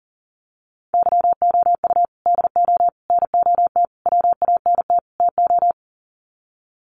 Morse code, heard as a keyed tone, about 40 wpm, 700 Hz. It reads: YOU DO NOT WANT TO
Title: Sets of 5 Words 40wpm